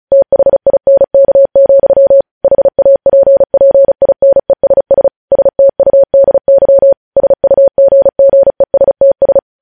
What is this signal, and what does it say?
Morse code, heard as a keyed tone, about 36 wpm, 550 Hz. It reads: THINK, HAPPINESS STUDY SUGGESTS
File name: podcast / News headlines-1x-35x35